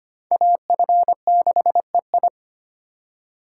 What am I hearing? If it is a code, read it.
Morse code, heard as a keyed tone, about 34 wpm, 700 Hz. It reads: AF6EI